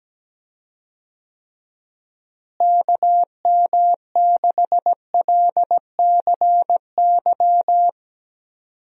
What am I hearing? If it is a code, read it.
Morse code, heard as a keyed tone, about 17 wpm, 700 Hz. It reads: KM6LCY
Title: USA Call Signs - 6 Characters - Repeated 2x - 17wpm